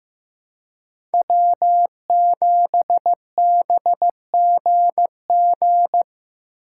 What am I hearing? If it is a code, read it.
Morse code, heard as a keyed tone, about 15 wpm, 700 Hz. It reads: W7BGG
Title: USA Call Signs - 5 Characters - 2x Character Spacing - 15wpm